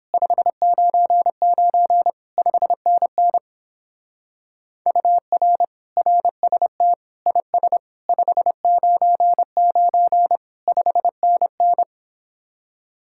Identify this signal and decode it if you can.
Morse code, heard as a keyed tone, about 29 wpm, 700 Hz. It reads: 599 5NN UR RST IS 599 5NN